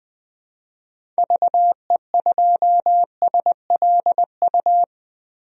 Morse code, heard as a keyed tone, about 20 wpm, 700 Hz. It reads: VE2SLU
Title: DXCC-100 Call Signs - 6 Characters - 20wpm